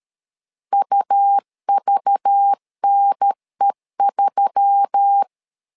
Morse code, heard as a keyed tone, about 13 wpm, 800 Hz. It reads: UVNE3
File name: LEZIONI / 5 LBVC / 5 LBV 12wpm 1855